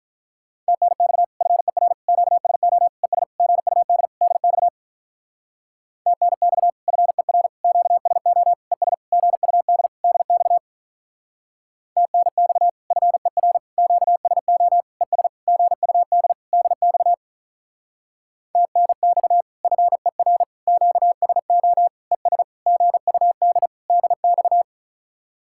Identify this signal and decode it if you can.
Morse code, heard as a keyed tone, about 43 wpm, 700 Hz. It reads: TNX FER QSO ES GUD DX TNX FER QSO ES GUD DX TNX FER QSO ES GUD DX TNX FER QSO ES GUD DX
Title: Speed-Racing Common QSO Phrases - 35wpm